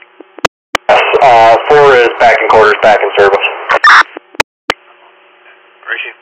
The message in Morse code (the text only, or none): none